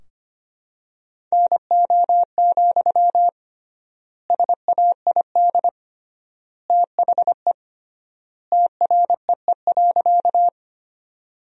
Morse code, heard as a keyed tone, about 25 wpm, 700 Hz. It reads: NO, SAID THE TREE.
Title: 3x Word-Spacing - Sentences from Top 300 Words - 25wpm